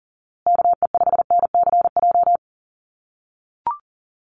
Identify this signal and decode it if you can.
Morse code, heard as a keyed tone, about 37 wpm, 700 Hz. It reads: KE5NCJ
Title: USA Call Signs Repeated 3x 40wpm